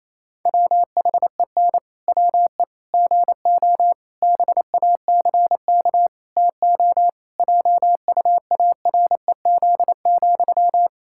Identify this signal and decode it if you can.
Morse code, heard as a keyed tone, about 28 wpm, 700 Hz. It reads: WHEN WE GO BACK TO JUAREZ,
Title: Pangram Challenge v2 - 28wpm